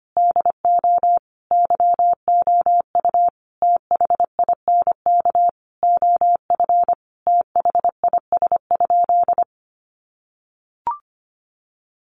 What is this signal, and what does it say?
Morse code, heard as a keyed tone, about 25 wpm, 700 Hz. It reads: DO YOU THINK OF THIS?
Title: Sentences from Top 200 Words 25wpm